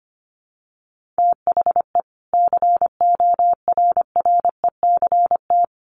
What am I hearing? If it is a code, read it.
Morse code, heard as a keyed tone, about 25 wpm, 700 Hz. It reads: THE CORRECT
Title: Sentences from Top 1400 Words 25wpm